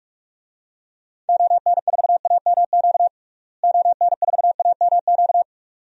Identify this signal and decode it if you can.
Morse code, heard as a keyed tone, about 46 wpm, 700 Hz. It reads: ON4AMQ ON4AMQ
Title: DXCC-100 Call Signs - 6 Characters - Repeated 2x - 45wpm